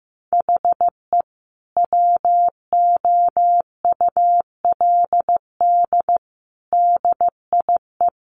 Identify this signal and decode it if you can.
Morse code, heard as a keyed tone, about 15 wpm, 700 Hz. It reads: HE WOULD DIE
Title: Sentences from Top 1300 Words 15wpm